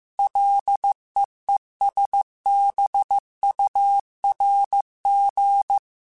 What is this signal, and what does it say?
Morse code, heard as a keyed tone, about 15 wpm, 800 Hz. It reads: LEESBURG